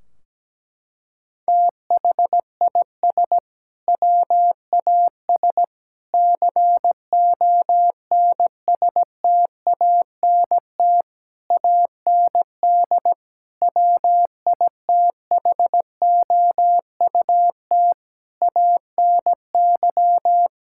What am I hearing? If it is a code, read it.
Morse code, heard as a keyed tone, about 17 wpm, 700 Hz. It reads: THIS WAS CONSTANT AND WITHOUT ANY
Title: Sentences from Top 1600 Words 17wpm